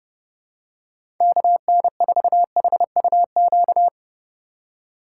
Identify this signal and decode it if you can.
Morse code, heard as a keyed tone, about 30 wpm, 700 Hz. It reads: KN4HUQ
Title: USA Call Signs - 6 Characters - 3x Character Spacing - 30wpm